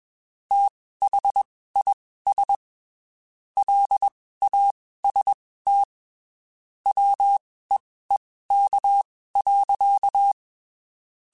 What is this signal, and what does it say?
Morse code, heard as a keyed tone, about 21 wpm, spaced as for 15 wpm, 800 Hz. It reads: THIS LAST WEEK.
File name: (spaced) / QSOs Spaced-mp3s / WD0FBT @20 sp